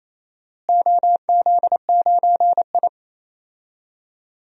Morse code, heard as a keyed tone, about 28 wpm, 700 Hz. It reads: OZ9I
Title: DXCC-100 Call Signs - 4 Characters - Repeated 3x - 28wpm